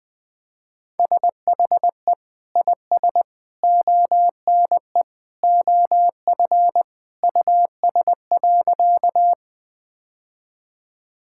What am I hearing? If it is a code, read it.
Morse code, heard as a keyed tone, about 21 wpm, 700 Hz. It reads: SHE IS ONE OF US.
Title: Sentences with 3-Word Phrases - 20wpm